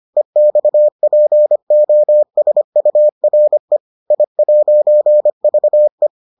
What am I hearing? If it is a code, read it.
Morse code, heard as a keyed tone, about 25 wpm, 600 Hz. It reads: EXPOSURE I'VE